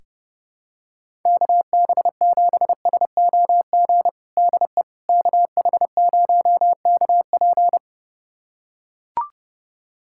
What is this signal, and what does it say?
Morse code, heard as a keyed tone, about 30 wpm, 700 Hz. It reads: KB7SOG DE KH0KP
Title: USA Callsign DE Callsign 30wpm